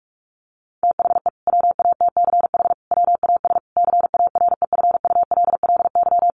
Morse code, heard as a keyed tone, about 44 wpm, 700 Hz. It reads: THE WATCH WAS CAREFULLY
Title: Sentences from Top 1300 Words 45wpm